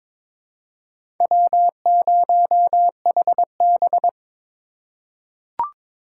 Morse code, heard as a keyed tone, about 22 wpm, 700 Hz. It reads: W0HB
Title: USA Call Signs - 3x Character Spacing - 22wpm